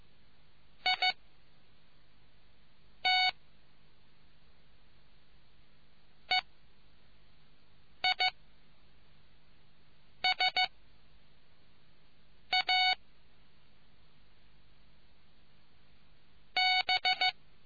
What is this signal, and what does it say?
Morse code, heard as a keyed tone, about 15 wpm, 740 Hz. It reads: I T E I S A B